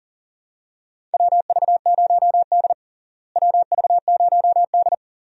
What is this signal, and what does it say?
Morse code, heard as a keyed tone, about 40 wpm, 700 Hz. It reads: WV0D WV0D
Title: USA Call Signs - 4 Characters - Repeated 3x - 40wpm